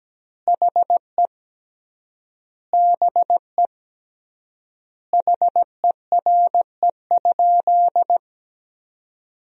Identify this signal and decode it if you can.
Morse code, heard as a keyed tone, about 17 wpm, 700 Hz. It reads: HE BE HERE?
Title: 3x Word-Spacing - Sentences from Top 400 Words - 17wpm